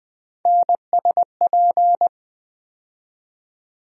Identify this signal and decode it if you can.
Morse code, heard as a keyed tone, about 20 wpm, 700 Hz. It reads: NSP
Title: USA Call Sign Postfixes - 3 Letters 20wpm